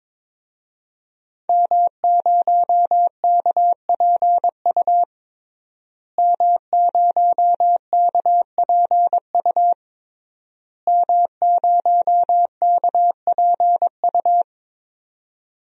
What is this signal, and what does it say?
Morse code, heard as a keyed tone, about 22 wpm, 700 Hz. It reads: M0KPU M0KPU M0KPU
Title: DXCC-100 Call Signs - 5 Characters - Repeated 3x - 22wpm